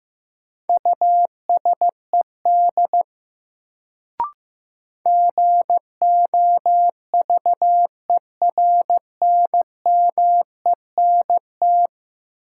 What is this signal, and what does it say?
Morse code, heard as a keyed tone, about 15 wpm, 700 Hz. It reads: USED GOVERNMENT
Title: Sentences from Top 600 Words - New Word Drill - 15wpm